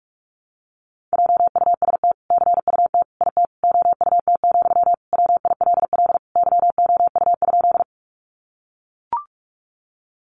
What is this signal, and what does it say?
Morse code, heard as a keyed tone, about 44 wpm, 700 Hz. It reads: JUST CUT IT OUT, WILL YOU?
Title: Sentences from Top 300 Words 45wpm